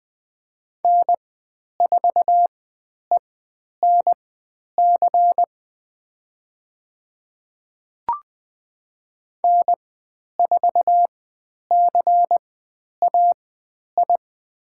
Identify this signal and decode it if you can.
Morse code, heard as a keyed tone, about 20 wpm, spaced as for 10 wpm, 700 Hz. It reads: N4ENC N4CAI
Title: 102 Lesson 17g - Letter u - Call Signs 20wpm